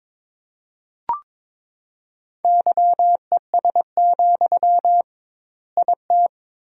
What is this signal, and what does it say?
Morse code, heard as a keyed tone, about 22 wpm, 700 Hz. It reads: YES, IT
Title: 2x Word-Spacing - Sentences from Top 500 Words - 22wpm